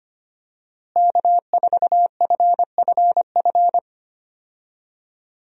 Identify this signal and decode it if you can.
Morse code, heard as a keyed tone, about 25 wpm, 700 Hz. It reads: K4FFF
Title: USA Call Signs - 5 Characters - Repeated 3x - 25wpm